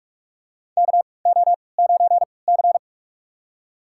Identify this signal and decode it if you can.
Morse code, heard as a keyed tone, about 46 wpm, spaced as for 26 wpm, 700 Hz. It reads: KO9C
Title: USA Call Signs - 4 Characters - 3x Character Spacing - 45wpm